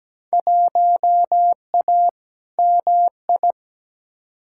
Morse code, heard as a keyed tone, about 17 wpm, 700 Hz. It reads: 1A MI
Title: Field Day Exchanges - 17wpm